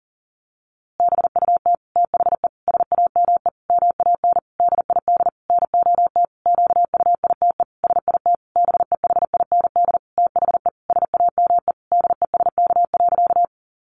Morse code, heard as a keyed tone, about 40 wpm, 700 Hz. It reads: BUT THE SAME MAN DID NOT QUITE SIT BEHIND THE SAME DESK.
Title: A Sound Of Thunder 40wpm